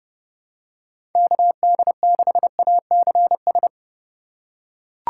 Morse code, heard as a keyed tone, about 30 wpm, 700 Hz. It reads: KD6ACS
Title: USA Call Signs - 6 Characters - 4x Character Spacing - 30wpm